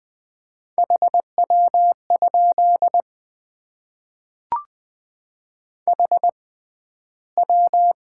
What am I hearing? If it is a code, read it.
Morse code, heard as a keyed tone, about 20 wpm, spaced as for 12 wpm, 700 Hz. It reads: HW? HW
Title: Top 90 QSO Elements - 6x Character Spacing - 20wpm